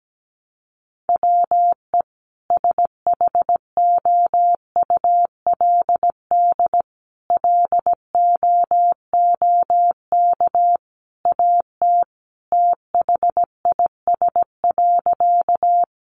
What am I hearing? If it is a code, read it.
Morse code, heard as a keyed tone, about 17 wpm, 700 Hz. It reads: WE SHOULD LOOK AT THIS.
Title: Sentences from Top 300 Words 17wpm